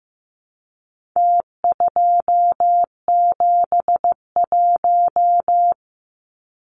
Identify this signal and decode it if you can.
Morse code, heard as a keyed tone, about 15 wpm, 700 Hz. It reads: T271
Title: Numbers - Contest Style - 4-Digit 15wpm